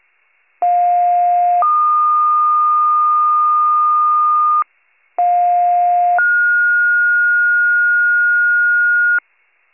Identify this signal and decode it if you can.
Two-tone page, heard as a one-second tone, then a three-second tone, about 710 then 1220 Hz, about 710 then 1430 Hz, with noise. It is heard from 0.6 to 9.2 s.